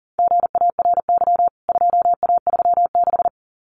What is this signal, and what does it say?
Morse code, heard as a keyed tone, about 40 wpm, 700 Hz. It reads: GARY 2A36
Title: CWT Exchange 40wpm